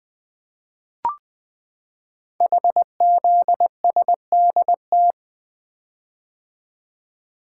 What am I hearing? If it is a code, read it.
Morse code, heard as a keyed tone, about 20 wpm, 700 Hz. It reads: HZSDT
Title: Code Groups - 5 Letters - 20wpm